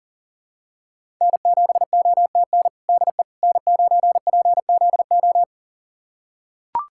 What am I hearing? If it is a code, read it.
Morse code, heard as a keyed tone, about 40 wpm, 700 Hz. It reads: N7OTN DE N9PZO
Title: USA Callsign DE Callsign - 5 Characters - 40wpm